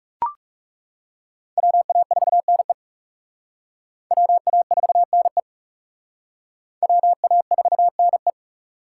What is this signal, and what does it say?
Morse code, heard as a keyed tone, about 35 wpm, 700 Hz. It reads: WA4NE WA4NE WA4NE
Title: Speed Racing - USA Call Signs 30wpm